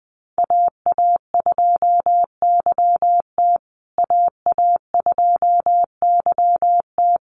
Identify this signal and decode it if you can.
Morse code, heard as a keyed tone, about 20 wpm, 700 Hz. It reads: AA2YT AA2YT
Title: USA Call Signs Repeated 3x 20wpm